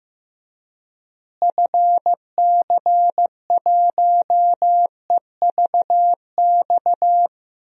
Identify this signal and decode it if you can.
Morse code, heard as a keyed tone, about 15 wpm, 700 Hz. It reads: FC1EVX
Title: DXCC-100 Call Signs - 6 Characters - Repeated 3x - 15wpm